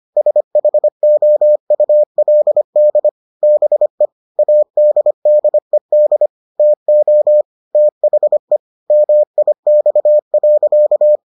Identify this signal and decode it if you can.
Morse code, heard as a keyed tone, about 25 wpm, 600 Hz. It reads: SHOULD BE ADDED TO THE MIX.